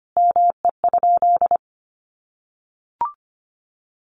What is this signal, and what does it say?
Morse code, heard as a keyed tone, about 25 wpm, 700 Hz. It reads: ME?
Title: Sentences from Top 1400 Words 25wpm